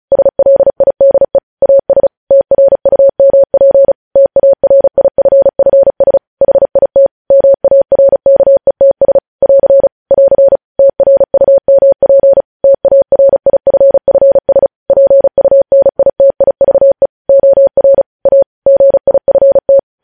Morse code, heard as a keyed tone, about 36 wpm, 550 Hz. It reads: SLIDE AS TRUMP TARIFFS HIT MARKETS <AR> <AR> TRUMP TARIFFS PUNITIVE OR A GIFT
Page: news headlines in morse code at 35 wpm